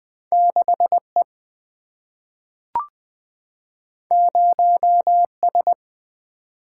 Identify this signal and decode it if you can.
Morse code, heard as a keyed tone, about 20 wpm, 700 Hz. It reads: 6E 0S
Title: Single-Letter Single-Number 20wpm